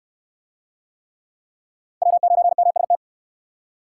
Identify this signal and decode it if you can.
Morse code, heard as a keyed tone, about 68 wpm, 700 Hz. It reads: W9MST